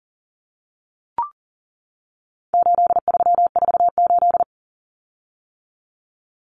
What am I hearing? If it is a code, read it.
Morse code, heard as a keyed tone, about 40 wpm, 700 Hz. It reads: 8348
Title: Common Cut Numbers - 4-Digit 40wpm